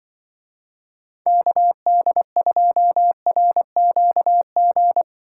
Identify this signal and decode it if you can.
Morse code, heard as a keyed tone, about 23 wpm, 700 Hz. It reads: KD2RQG
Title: Speed Racing - USA 2x3 Call Signs 20wpm